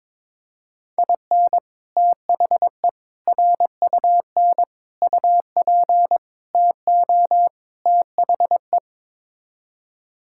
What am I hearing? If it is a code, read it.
Morse code, heard as a keyed tone, about 22 wpm, 700 Hz. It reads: IN THE RUN UP TO THE
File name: practice / 22wpm / Sets of 6 Words - 22wpm